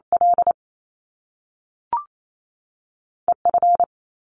Morse code, heard as a keyed tone, about 28 wpm, 700 Hz. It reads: L E F
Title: Top Two Letter Combinations in Words 28wpm